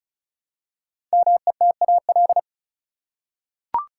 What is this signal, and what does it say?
Morse code, heard as a keyed tone, about 36 wpm, 700 Hz. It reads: METAL
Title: Five Letter Words - 35wpm